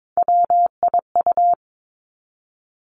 Morse code, heard as a keyed tone, about 22 wpm, 700 Hz. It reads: WIU
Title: Three Challenging Letter-Number Combinations 22wpm